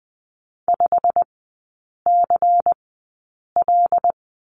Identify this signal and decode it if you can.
Morse code, heard as a keyed tone, about 20 wpm, spaced as for 8 wpm, 700 Hz. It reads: 5CL
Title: 093 Lesson 16e - Letter c - Three Characters 20wpm